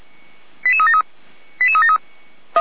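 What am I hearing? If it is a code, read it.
Five-tone selective call (ZVEI), heard as quick runs of five tones, tones of about 70 ms. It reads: 88373 88373
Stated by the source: ZVEI five-tone call 88012 received on SDR, recorded by fire brigade Feuerwehr Orlamünde.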